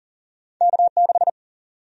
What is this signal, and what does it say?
Morse code, heard as a keyed tone, about 40 wpm, 700 Hz. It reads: K6
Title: USA Call Sign Prefixes - 40wpm